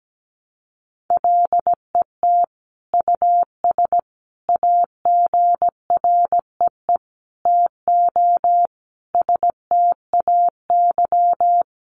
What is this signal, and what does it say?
Morse code, heard as a keyed tone, about 17 wpm, 700 Hz. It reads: LET US AGREE TO STAY
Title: Sentences from Top 1100 Words 17wpm